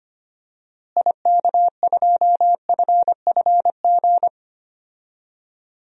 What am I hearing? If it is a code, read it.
Morse code, heard as a keyed tone, about 25 wpm, 700 Hz. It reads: IK2FFG
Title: DXCC-100 Call Signs - 6 Characters - Repeated 2x - 25wpm